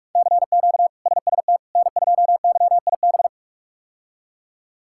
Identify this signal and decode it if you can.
Morse code, heard as a keyed tone, about 41 wpm, 700 Hz. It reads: CQ SST N2YIB
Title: K1USN SST QSO Practice - 45wpm